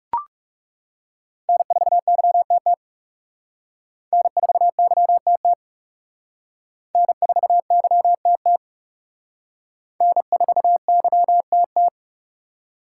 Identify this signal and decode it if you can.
Morse code, heard as a keyed tone, about 36 wpm, 700 Hz. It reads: N4YTT N4YTT N4YTT N4YTT
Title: Speed Racing - USA 1x3 Call Signs 30wpm